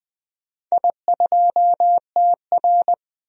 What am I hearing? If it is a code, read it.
Morse code, heard as a keyed tone, about 20 wpm, 700 Hz. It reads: I2TR
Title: DXCC-100 Call Signs - 4 Characters - 20wpm